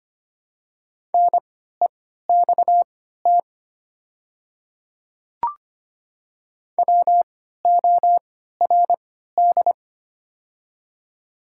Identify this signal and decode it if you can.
Morse code, heard as a keyed tone, about 25 wpm, spaced as for 14 wpm, 700 Hz. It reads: NEXT WORD
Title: 3x Character Spacing - Top 300 Words - 25wpm